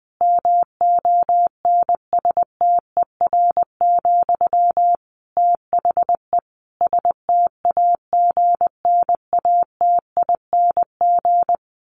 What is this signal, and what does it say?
Morse code, heard as a keyed tone, about 20 wpm, 700 Hz. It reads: MONSTER, THE STAGNATING
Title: A Sound Of Thunder 20wpm